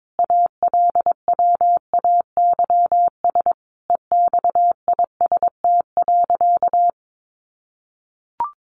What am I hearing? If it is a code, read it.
Morse code, heard as a keyed tone, about 22 wpm, 700 Hz. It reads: ALWAYS EXIST.
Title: Sentences from Top 1500 Words 22wpm